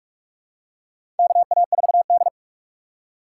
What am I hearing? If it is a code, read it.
Morse code, heard as a keyed tone, about 44 wpm, 700 Hz. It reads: KA4D
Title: USA Call Signs - 3x Character Spacing - 45wpm